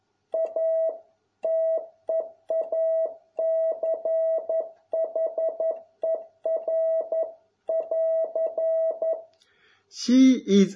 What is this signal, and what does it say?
Morse code, heard as a keyed tone, about 11 wpm, 650 Hz. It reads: A TEACHER<AR>